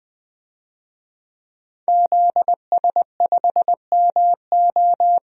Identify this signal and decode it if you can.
Morse code, heard as a keyed tone, about 20 wpm, 700 Hz. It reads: ZS5MO
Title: DXCC-100 Call Signs - 5 Characters - 20wpm